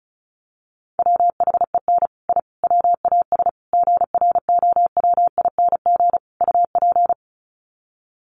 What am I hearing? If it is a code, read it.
Morse code, heard as a keyed tone, about 34 wpm, 700 Hz. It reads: WHEN I WAS GROWING UP
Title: Sets of 5 Words Encore 35wpm